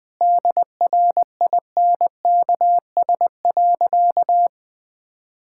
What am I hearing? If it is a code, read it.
Morse code, heard as a keyed tone, about 20 wpm, 700 Hz. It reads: DRINKS.
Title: Sentences with 6-Word Phrases - 20wpm